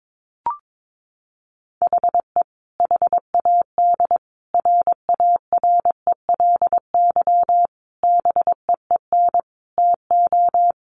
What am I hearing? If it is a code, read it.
Morse code, heard as a keyed tone, about 22 wpm, 700 Hz. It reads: HE HAD RARELY BEEN TO
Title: Sentences from Top 1400 Words 22wpm